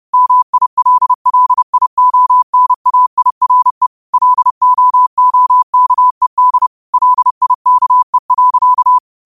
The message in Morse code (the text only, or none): MILLIONAIRE LOOKED LIKE.